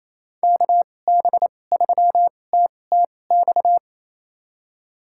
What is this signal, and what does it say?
Morse code, heard as a keyed tone, about 28 wpm, spaced as for 20 wpm, 700 Hz. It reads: KB3TTX
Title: USA Call Signs - 6 Characters - 2x Character Spacing - 28wpm